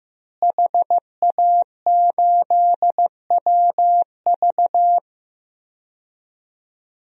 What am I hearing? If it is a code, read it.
Morse code, heard as a keyed tone, about 15 wpm, 700 Hz. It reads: HA8WV
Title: DXCC-100 Call Signs - 5 Characters - Repeated 2x - 15wpm